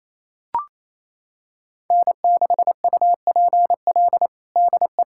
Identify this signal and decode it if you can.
Morse code, heard as a keyed tone, about 28 wpm, 700 Hz. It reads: N6UPL DE
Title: USA Callsign DE Callsign - 5 Characters - 28wpm